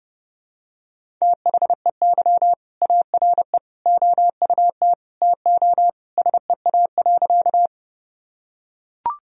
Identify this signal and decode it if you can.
Morse code, heard as a keyed tone, about 32 wpm, 700 Hz. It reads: THEY ARE OUT TO SEA.
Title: Sentences from Top 1000 Words - Review Part 1 - 30wpm